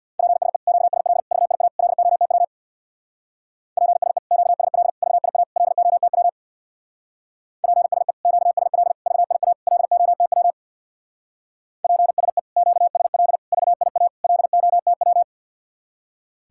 Morse code, heard as a keyed tone, about 62 wpm, 700 Hz. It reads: PSE QSL VIA LOTW PSE QSL VIA LOTW PSE QSL VIA LOTW PSE QSL VIA LOTW
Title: Speed-Racing Common QSO Phrases - 50wpm